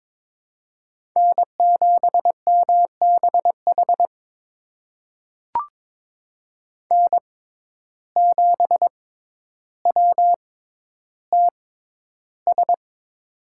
Morse code, heard as a keyed tone, about 22 wpm, spaced as for 11 wpm, 700 Hz. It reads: N7MBH N7WTS
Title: USA Call Signs - 5 Characters - 6x Character Spacing - 22wpm